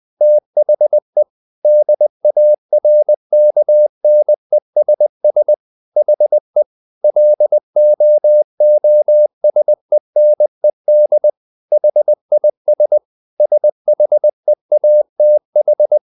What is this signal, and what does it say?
Morse code, heard as a keyed tone, about 20 wpm, 600 Hz. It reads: THE DARKNESS HE LOOSENED HIS SHEATH
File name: the old man and the sea / the old man and the sea part 1 20wpm